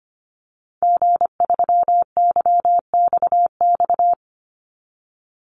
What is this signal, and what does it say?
Morse code, heard as a keyed tone, about 25 wpm, 700 Hz. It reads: G3YXX
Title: Call Signs - 25wpm